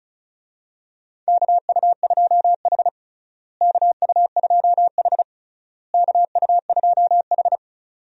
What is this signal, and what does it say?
Morse code, heard as a keyed tone, about 34 wpm, 700 Hz. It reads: KU2H KU2H KU2H
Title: USA Call Signs - 4 Characters - Repeated 3x - 35wpm